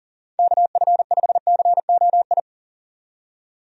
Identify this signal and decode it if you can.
Morse code, heard as a keyed tone, about 40 wpm, 700 Hz. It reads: KF5COI
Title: USA Call Signs - 6 Characters - 40wpm